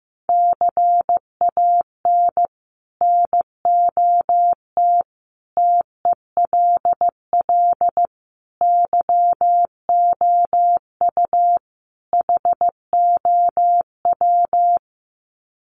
Morse code, heard as a keyed tone, about 15 wpm, 700 Hz. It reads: CAN NOT TELL YOU HOW
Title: Sets of 5 Words Encore2 15wpm